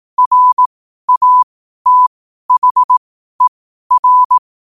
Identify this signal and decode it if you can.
Morse code, heard as a keyed tone, about 18 wpm, spaced as for 12 wpm, 1000 Hz. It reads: RATHER
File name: qotdfiles / 12wpm